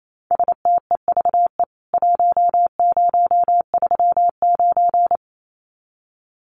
Morse code, heard as a keyed tone, about 28 wpm, 700 Hz. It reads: STEVE 1039
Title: CWT Exchange 28wpm